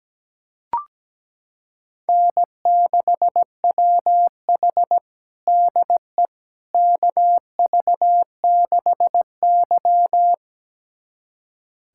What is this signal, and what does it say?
Morse code, heard as a keyed tone, about 17 wpm, 700 Hz. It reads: N6WH DE KV6Y
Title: USA Callsign DE Callsign - 4 Characters - 17wpm